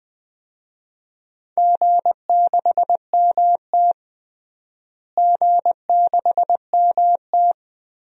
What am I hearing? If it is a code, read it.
Morse code, heard as a keyed tone, about 21 wpm, 700 Hz. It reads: G6MT G6MT
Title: DXCC-100 Call Signs - 4 Characters - Repeated 3x - 20wpm